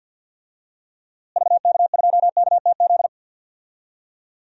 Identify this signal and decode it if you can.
Morse code, heard as a keyed tone, about 50 wpm, 700 Hz. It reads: VK2KTZ
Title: DXCC-100 Call Signs - 6 Characters - Repeated 2x - 50wpm